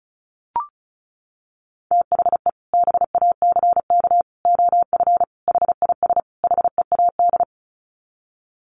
Morse code, heard as a keyed tone, about 34 wpm, 700 Hz. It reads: THE BACK OF HIS HEAD
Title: Sets of 5 Words Encore 35wpm